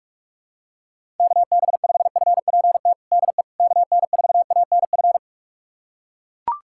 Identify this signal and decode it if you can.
Morse code, heard as a keyed tone, about 41 wpm, 700 Hz. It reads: KB5FPT DE KN4ANF
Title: USA Callsign DE Callsign - 6 Characters - 45wpm